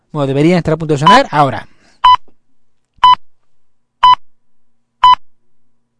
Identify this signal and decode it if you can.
Radio time signal (time pips), with speech. It is heard from 1.1 to 5.2 s.